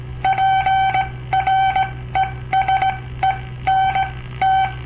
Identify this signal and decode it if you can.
Morse code, heard as a keyed tone, about 18 wpm, spaced as for 14 wpm, 770 Hz. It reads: PRESENT
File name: papertapes / local / Code Practice-01